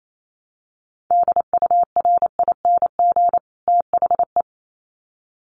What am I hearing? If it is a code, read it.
Morse code, heard as a keyed tone, about 28 wpm, 700 Hz. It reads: DURING THE
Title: Sets of 2 Words 28wpm